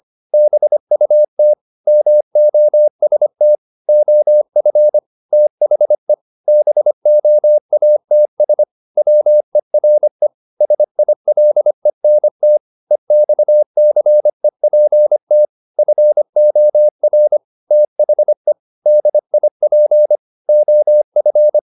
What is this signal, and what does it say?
Morse code, heard as a keyed tone, about 25 wpm, 600 Hz. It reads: BUT MOST OF THE BOATS WERE SILENT EXCEPT FOR THE DIP OF
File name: the old man and the sea part 1 25wpm